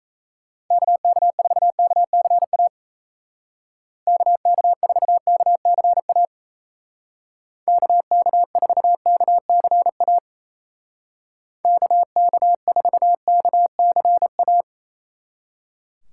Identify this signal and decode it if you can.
Morse code, heard as a keyed tone, about 34 wpm, 700 Hz. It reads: KK4KCA KK4KCA KK4KCA KK4KCA
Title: Speed Racing - USA Call Signs 28wpm